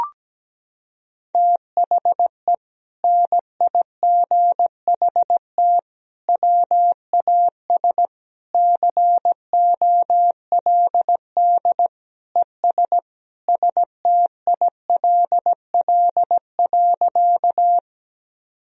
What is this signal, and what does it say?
Morse code, heard as a keyed tone, about 17 wpm, 700 Hz. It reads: THE NIGHT WAS COLD ES STILL.
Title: Sentences from Top 400 Words and Abbreviations 17wpm